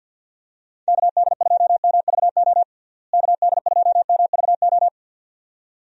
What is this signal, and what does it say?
Morse code, heard as a keyed tone, about 52 wpm, 700 Hz. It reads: KD2MVO KD2MVO
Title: USA Call Signs - 6 Characters - Repeated 2x - 50wpm